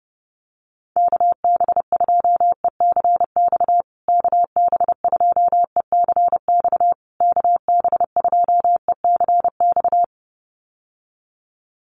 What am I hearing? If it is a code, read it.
Morse code, heard as a keyed tone, about 30 wpm, 700 Hz. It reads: KB2ECX KB2ECX KB2ECX
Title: USA Call Signs Repeated 3x 30wpm